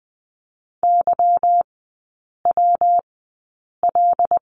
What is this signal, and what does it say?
Morse code, heard as a keyed tone, about 20 wpm, spaced as for 8 wpm, 690 Hz. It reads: YWL
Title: 198 Lesson 32e - Letter k - Three Characters 20wpm